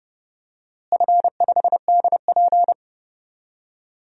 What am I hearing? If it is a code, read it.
Morse code, heard as a keyed tone, about 30 wpm, 700 Hz. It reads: F5DP
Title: DXCC-100 Call Signs - 4 Characters - Repeated 3x - 30wpm